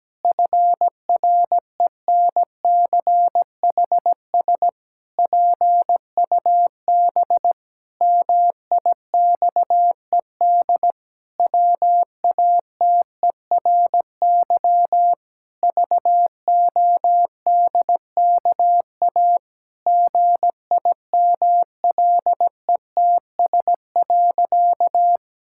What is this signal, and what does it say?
Morse code, heard as a keyed tone, about 17 wpm, 700 Hz. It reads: FRENCHS PUB MIXED WATERY VODKA GIMLETS.
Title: Pangram Challenge v2 - 17wpm